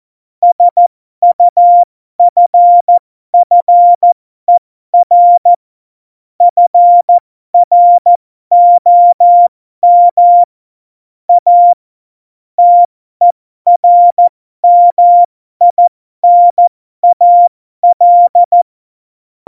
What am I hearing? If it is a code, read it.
Morse code, heard as a keyed tone, about 14 wpm, spaced as for 12 wpm, 700 Hz. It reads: SUFFER FROM A TERMINAL
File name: Morsekurs / Witze